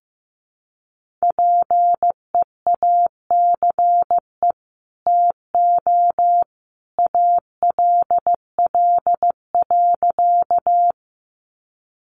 Morse code, heard as a keyed tone, about 15 wpm, 700 Hz. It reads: PEACE TO ALL.
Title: Sentences from Top 1500 Words 15wpm